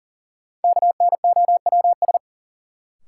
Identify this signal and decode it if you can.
Morse code, heard as a keyed tone, about 40 wpm, 700 Hz. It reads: KNOWS
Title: Sentences from Top 1000 Words - Review Part 1 - New Word Drill - 40wpm